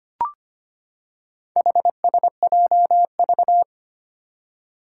Morse code, heard as a keyed tone, about 25 wpm, 700 Hz. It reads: HSJV
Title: Four Challenging Letter-Number Combinations 25wpm